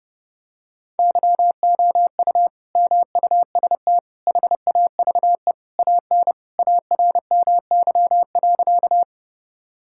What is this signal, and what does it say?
Morse code, heard as a keyed tone, about 32 wpm, 700 Hz. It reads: YOU MUST HAVE AN ARMY.
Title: Sentences from Top 1000 Words - Review Part 5 - 30wpm